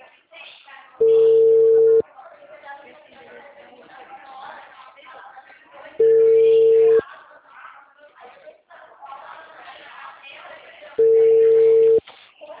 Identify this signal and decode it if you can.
Ringback tone (China, 450 Hz), with speech.
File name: out- -2014- - - .41780